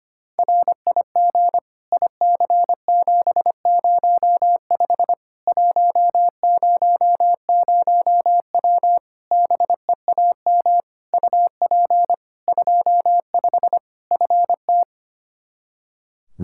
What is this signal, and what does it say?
Morse code, heard as a keyed tone, about 25 wpm, 700 Hz. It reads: RIG IC705 100W BEAM UP 25 FT
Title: K1USN SST QSO Practice 25wpm